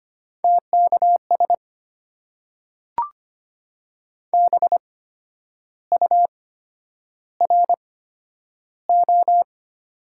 Morse code, heard as a keyed tone, about 25 wpm, spaced as for 10 wpm, 700 Hz. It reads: TKS BURO
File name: Top 90 QSO Elements - 8x Character Spacing - 25wpm